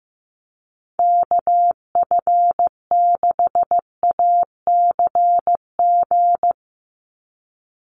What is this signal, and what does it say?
Morse code, heard as a keyed tone, about 15 wpm, 700 Hz. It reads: KF6ACG
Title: USA Call Signs 15wpm